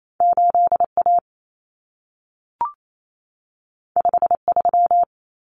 Morse code, heard as a keyed tone, about 28 wpm, 700 Hz. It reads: 8A 53
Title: Common Cut Numbers - 2-Digit 28wpm